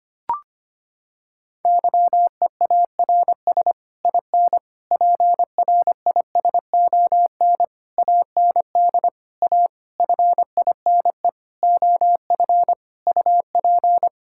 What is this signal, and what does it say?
Morse code, heard as a keyed tone, about 25 wpm, 700 Hz. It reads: YEARS IN PRISON AND A FINE OF UP
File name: Sets of 9 Words - 25wpm